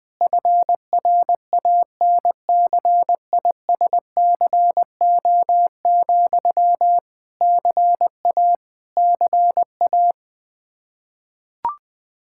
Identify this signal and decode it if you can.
Morse code, heard as a keyed tone, about 20 wpm, 700 Hz. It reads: FRANCISCO, CA CA
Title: Practice QSO - Exchange 1 - Beginning - 20wpm